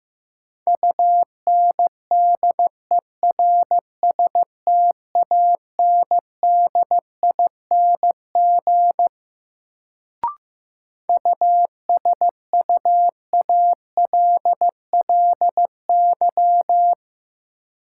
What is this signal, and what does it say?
Morse code, heard as a keyed tone, about 15 wpm, 700 Hz. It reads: UNDERSTANDING USUALLY
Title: Sentences from Top 1000 Words - Review Part 3 - New Word Drill - 15wpm